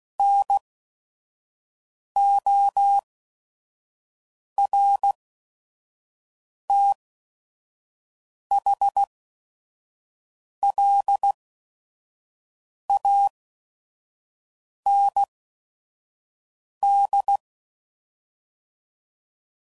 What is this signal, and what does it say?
Morse code, heard as a keyed tone, about 16 wpm, spaced as for 5 wpm, 800 Hz. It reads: NORTHLAND
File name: CW Code Groups @5WPM / fromSM's / Cities